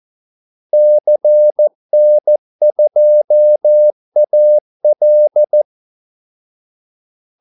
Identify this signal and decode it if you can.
Morse code, heard as a keyed tone, about 14 wpm, 600 Hz. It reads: CN2AL
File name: Callsigns-14wpm